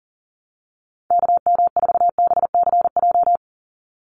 Morse code, heard as a keyed tone, about 40 wpm, 700 Hz. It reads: KM4BCJ
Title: USA Call Signs 40wpm